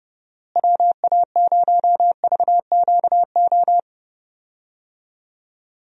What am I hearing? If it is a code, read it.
Morse code, heard as a keyed tone, about 30 wpm, 700 Hz. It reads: WA0VQO
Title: USA Call Signs - 6 Characters - Repeated 3x - 30wpm